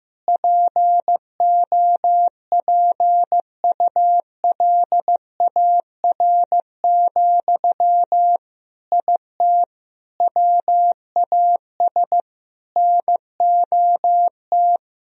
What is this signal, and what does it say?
Morse code, heard as a keyed tone, about 15 wpm, 700 Hz. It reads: POPULAR, IT WAS NOT
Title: Sentences from Top 1000 Words - Review Part 3 - 15wpm